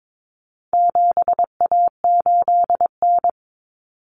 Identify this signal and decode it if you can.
Morse code, heard as a keyed tone, about 22 wpm, 700 Hz. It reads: 7A8N
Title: Common Cut Numbers - 4-Digit 22wpm